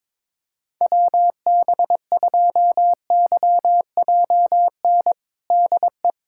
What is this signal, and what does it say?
Morse code, heard as a keyed tone, about 22 wpm, 700 Hz. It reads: WB2YJN DE
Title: USA Callsign DE Callsign - 6 Characters - 22wpm